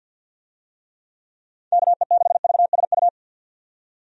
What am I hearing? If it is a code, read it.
Morse code, heard as a keyed tone, about 50 wpm, 700 Hz. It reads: KE6VSU
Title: USA Call Signs - 6 Characters - 50wpm